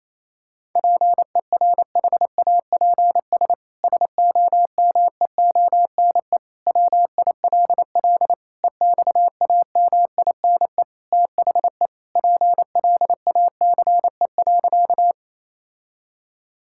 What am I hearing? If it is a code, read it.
Morse code, heard as a keyed tone, about 28 wpm, 700 Hz. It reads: PERHAPS SOMEONE WILL EXAMINE THE PLACE.
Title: Sentences from Top 1600 Words 28wpm